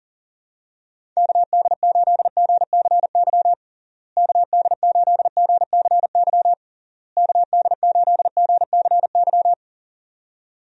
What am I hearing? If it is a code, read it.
Morse code, heard as a keyed tone, about 37 wpm, 700 Hz. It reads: KD8GCY KD8GCY KD8GCY
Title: USA Call Signs - 6 Characters - Repeated 3x - 40wpm